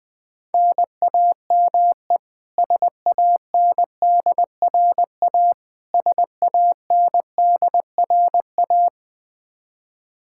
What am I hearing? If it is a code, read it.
Morse code, heard as a keyed tone, about 20 wpm, 700 Hz. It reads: NAME SANDRA SANDRA
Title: Practice QSO - Exchange 1 - Beginning - 20wpm